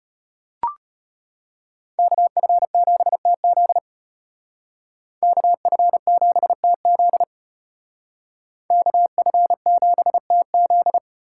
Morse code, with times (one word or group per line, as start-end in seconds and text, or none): 2.0-3.8 KF7TZ
5.2-7.2 KF7TZ
8.7-11.0 KF7TZ